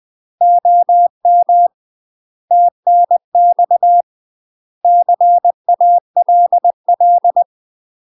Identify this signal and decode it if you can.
Morse code, heard as a keyed tone, about 20 wpm, 700 Hz. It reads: OM TNX CALL
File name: UK MASTER QSOs TRAINING / 20WPM / UK MASTER QSOs 0001